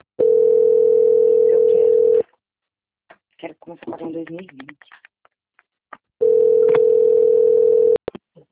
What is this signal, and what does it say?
Ringback tone (North America), 440 + 480 Hz, with speech.
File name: out- - - - - .47262